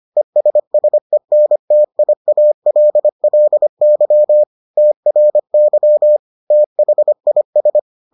Morse code, heard as a keyed tone, about 25 wpm, 600 Hz. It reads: ESSENTIALLY TRY THIS